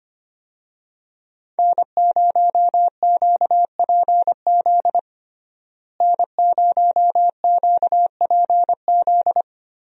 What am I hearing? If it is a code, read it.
Morse code, heard as a keyed tone, about 25 wpm, 700 Hz. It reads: N0QPZ N0QPZ
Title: USA Call Signs - 5 Characters - Repeated 3x - 25wpm